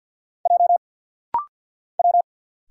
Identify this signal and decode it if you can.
Morse code, heard as a keyed tone, about 50 wpm, 700 Hz. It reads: J W
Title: Single Letters - Rapid-Fire 50wpm